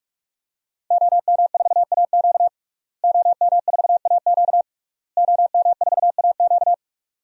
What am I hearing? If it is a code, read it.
Morse code, heard as a keyed tone, about 41 wpm, 700 Hz. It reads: OM4AQ OM4AQ OM4AQ
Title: DXCC-100 Call Signs - 5 Characters - Repeated 3x - 45wpm